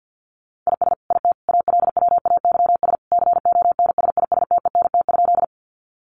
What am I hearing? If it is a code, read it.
Morse code, heard as a keyed tone, about 50 wpm, 700 Hz. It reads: IS IT ALWAYS CONSISTENT?
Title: Sentences from Top 1500 Words 50wpm